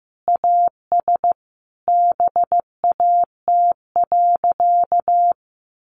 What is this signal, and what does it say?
Morse code, heard as a keyed tone, about 15 wpm, 700 Hz. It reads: AS BAT.
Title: Sentences from Top 1500 Words 15wpm